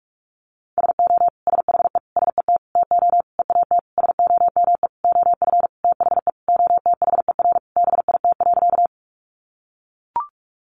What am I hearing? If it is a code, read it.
Morse code, heard as a keyed tone, about 44 wpm, 700 Hz. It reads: SO SHE SET TO EAT SOME OF THE OTHER BIT.
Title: Sentences from Top 900 Words 45wpm